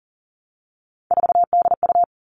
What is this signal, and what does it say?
Morse code, heard as a keyed tone, about 40 wpm, 700 Hz. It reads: 4DU
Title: Three Challenging Letter-Number Combinations 40wpm